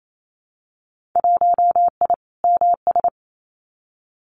Morse code, heard as a keyed tone, about 28 wpm, 700 Hz. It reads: 1I MS
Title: Winter Field Day Exchanges Encore 28wpm